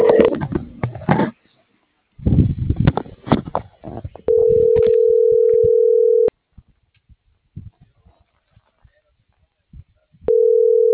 Ringback tone (North America), 440 + 480 Hz.